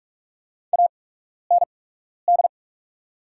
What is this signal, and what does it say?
Morse code, heard as a keyed tone, about 44 wpm, spaced as for 12 wpm, 690 Hz. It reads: AND
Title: 8x Character Spacing - Top 300 Words - 45wpm